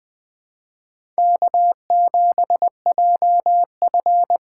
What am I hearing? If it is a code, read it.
Morse code, heard as a keyed tone, about 21 wpm, 700 Hz. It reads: K7JF
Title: USA Call Signs - 2x Character Spacing - 20wpm